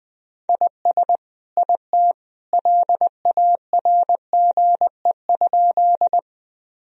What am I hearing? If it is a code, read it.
Morse code, heard as a keyed tone, about 20 wpm, 700 Hz. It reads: IS IT LARGE?
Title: Sentences from Top 1000 Words - Review Part 1 - 20wpm